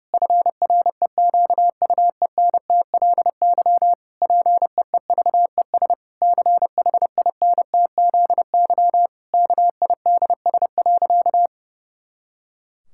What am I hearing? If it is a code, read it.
Morse code, heard as a keyed tone, about 30 wpm, 700 Hz. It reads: FREQUENTLY PEEVES CHINTZY KIDS.
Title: Pangram Challenge v2 - 30wpm